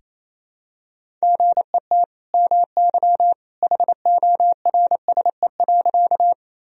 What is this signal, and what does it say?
Morse code, heard as a keyed tone, about 28 wpm, 700 Hz. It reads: GET MY HORSE.
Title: Sentences from Top 1000 Words - Review Part 1 - 28wpm